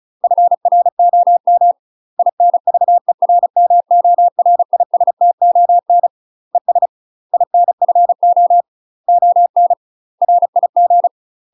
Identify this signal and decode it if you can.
Morse code, heard as a keyed tone, about 36 wpm, 700 Hz. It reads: FROM INVERMORISTON ES INFO ON RIG
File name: UK MASTER QSOs TRAINING / 35WPM / UK MASTER QSOs 0033